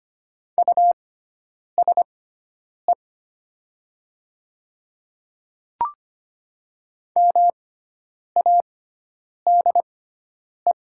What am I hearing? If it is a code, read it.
Morse code, heard as a keyed tone, about 25 wpm, spaced as for 9 wpm, 700 Hz. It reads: USE MADE
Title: 6x Character Spacing - Top 200 Words - 25wpm